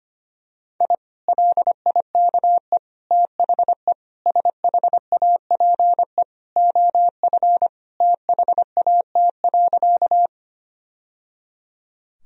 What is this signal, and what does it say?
Morse code, heard as a keyed tone, about 25 wpm, 700 Hz. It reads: I LIKE THE SHAPE OF THAT.
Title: Sentences from Top 1000 Words - Review Part 3 - 25wpm